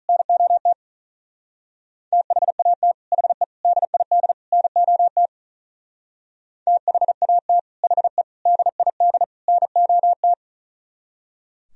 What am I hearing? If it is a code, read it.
Morse code, heard as a keyed tone, about 46 wpm, 700 Hz. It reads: NOT THAT HE DID NOT THAT HE DID NOT